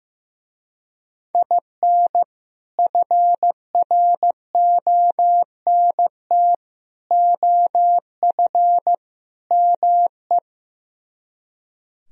Morse code, heard as a keyed tone, about 15 wpm, 700 Hz. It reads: IN FRONT OF ME
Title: Speed-Racing - Sets of 4 Words 15wpm